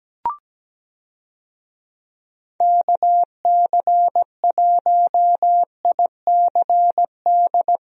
Morse code, heard as a keyed tone, about 17 wpm, 700 Hz. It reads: KC1ICD
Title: USA Call Signs - 6 Characters - Repeated 2x - 17wpm